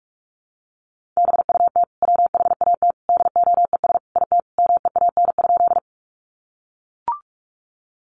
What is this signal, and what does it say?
Morse code, heard as a keyed tone, about 44 wpm, 700 Hz. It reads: BUT WHAT DOES IT MEAN?
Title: Sentences from Top 200 Words 45wpm